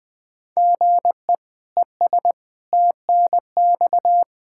Morse code, heard as a keyed tone, about 20 wpm, 700 Hz. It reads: GE ES TNX
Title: Practice QSO - Exchange 1 - Beginning - 20wpm